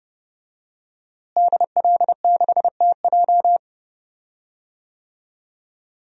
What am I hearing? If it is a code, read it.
Morse code, heard as a keyed tone, about 32 wpm, 700 Hz. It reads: DL6TJ
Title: DXCC-100 Call Signs - 5 Characters - Repeated 3x - 30wpm